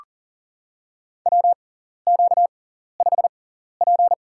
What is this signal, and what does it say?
Morse code, heard as a keyed tone, about 40 wpm, spaced as for 14 wpm, 700 Hz. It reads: WQ5P
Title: USA Call Signs - 6x Character Spacing - 40wpm